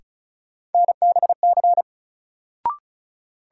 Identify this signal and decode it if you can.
Morse code, heard as a keyed tone, about 34 wpm, 700 Hz. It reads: NBC
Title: Sets of 5 Words Encore - New Word Drill - 35wpm